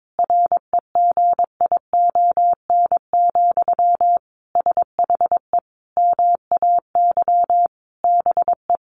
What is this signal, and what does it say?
Morse code, heard as a keyed tone, about 22 wpm, 700 Hz. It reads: REGION, SHE MAY BE
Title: Sentences from Top 1500 Words 22wpm